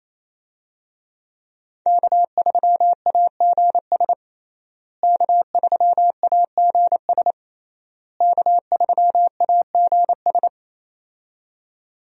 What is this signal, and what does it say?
Morse code, heard as a keyed tone, about 27 wpm, 700 Hz. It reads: K3AGS K3AGS K3AGS
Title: USA Call Signs - 5 Characters - Repeated 3x - 28wpm